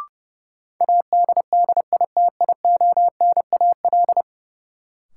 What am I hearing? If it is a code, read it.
Morse code, heard as a keyed tone, about 30 wpm, 700 Hz. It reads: ADDITIONAL
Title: Sentences from Top 1400 Words - New Word Drill Alt - 30wpm